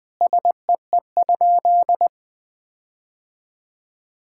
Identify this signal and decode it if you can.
Morse code, heard as a keyed tone, about 20 wpm, 700 Hz. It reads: SEE?
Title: Sentences with 5-Word Phrases - 20wpm